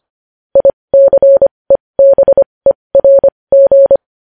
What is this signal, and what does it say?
Morse code, heard as a keyed tone, about 25 wpm, spaced as for 20 wpm, 550 Hz. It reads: ICEBERG
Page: news-headlines-in-morse-code-at-20-wpm